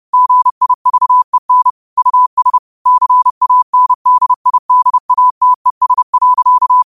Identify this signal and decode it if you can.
Morse code, heard as a keyed tone, about 30 wpm, 1000 Hz. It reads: GIVEN US CANDIDATES.
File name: qotdfiles / 30wpm